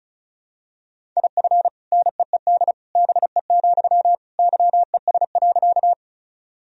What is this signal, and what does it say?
Morse code, heard as a keyed tone, about 36 wpm, 700 Hz. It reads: IF NEED BE, YES.
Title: Sentences from Top 1000 Words - Review Part 1 - 35wpm